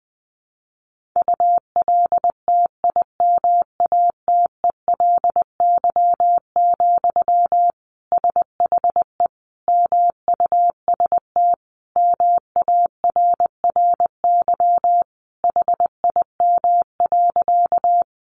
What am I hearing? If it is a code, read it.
Morse code, heard as a keyed tone, about 20 wpm, 700 Hz. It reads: ULTIMATELY, SHE MUST MARRY HIM.
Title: Sentences from Top 1200 Words 20wpm